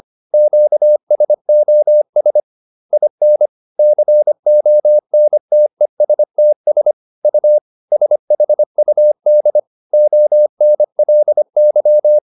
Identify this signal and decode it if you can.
Morse code, heard as a keyed tone, about 25 wpm, 600 Hz. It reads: QSOS IN CONTESTS U SHUD ONLY